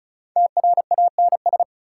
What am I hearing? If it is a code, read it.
Morse code, heard as a keyed tone, about 36 wpm, 700 Hz. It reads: TRANS
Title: Prefixes and Suffixes 35wpm